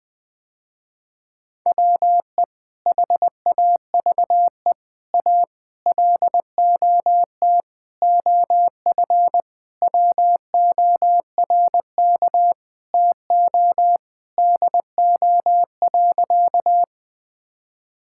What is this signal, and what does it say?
Morse code, heard as a keyed tone, about 19 wpm, 700 Hz. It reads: WE HAVE A LOT OF WORK TO DO.
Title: Sentences with 5-Word Phrases - Spoken Beginning - 20wpm